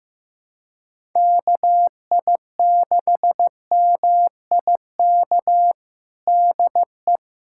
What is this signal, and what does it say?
Morse code, heard as a keyed tone, about 16 wpm, 700 Hz. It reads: KI6MIK DE
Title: USA Callsign DE Callsign - 6 Characters - 15wpm